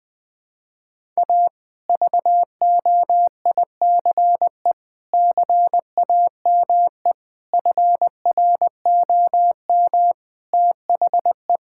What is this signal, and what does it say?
Morse code, heard as a keyed tone, about 20 wpm, 700 Hz. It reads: A VOICE CAME FROM THE
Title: Sentences from Top 1000 Words - Review Part 5 - 20wpm